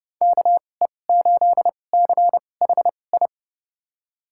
Morse code, heard as a keyed tone, about 30 wpm, spaced as for 22 wpm, 700 Hz. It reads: KE8CHI
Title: USA Call Signs - 6 Characters - 2x Character Spacing - 30wpm